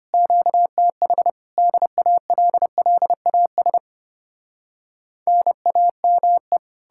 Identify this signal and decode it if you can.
Morse code, heard as a keyed tone, about 33 wpm, 700 Hz. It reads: QTH DALLAS NAME